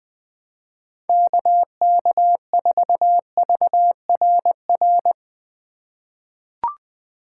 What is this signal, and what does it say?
Morse code, heard as a keyed tone, about 20 wpm, 700 Hz. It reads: KK4VRR
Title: USA Call Signs - 6 Characters - 4x Character Spacing - 20wpm